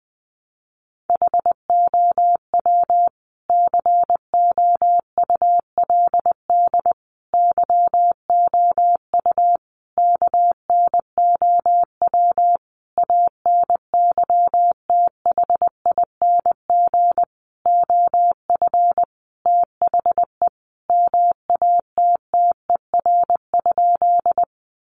Morse code, heard as a keyed tone, about 20 wpm, 700 Hz. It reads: HOW COULD YOU KNOW ANYTHING OF THE MATTER?
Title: Sentences from Top 1000 Words 20wpm